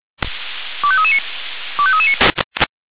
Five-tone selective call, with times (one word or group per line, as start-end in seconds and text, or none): 0.8-1.2 25339
1.8-2.1 25339